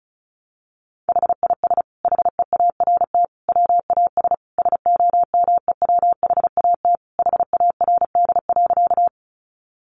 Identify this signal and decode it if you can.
Morse code, heard as a keyed tone, about 34 wpm, 700 Hz. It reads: HIS HEART WAS SOMEWHAT HARD.
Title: Sentences from Top 1300 Words 35wpm